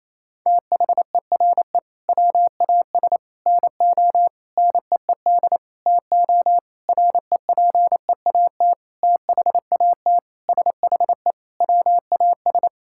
Morse code, heard as a keyed tone, about 28 wpm, 700 Hz. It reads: THERE WAS NO NEED TO REPEAT THAT SHE WAS
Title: Sentences from Top 1600 Words 28wpm